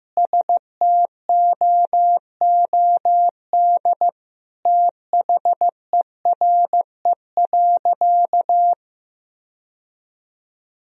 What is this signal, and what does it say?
Morse code, heard as a keyed tone, about 16 wpm, 700 Hz. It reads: STOOD THERE.
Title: Sentences from Top 1000 Words - Review Part 2 - 15wpm